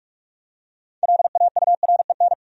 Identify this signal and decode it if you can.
Morse code, heard as a keyed tone, about 46 wpm, 700 Hz. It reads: LAUREN
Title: USA Common Names - 45wpm